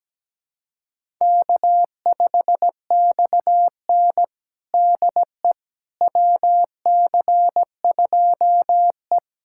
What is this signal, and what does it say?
Morse code, heard as a keyed tone, about 17 wpm, 700 Hz. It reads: K5XN DE WC2E
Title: USA Callsign DE Callsign - 4 Characters - 17wpm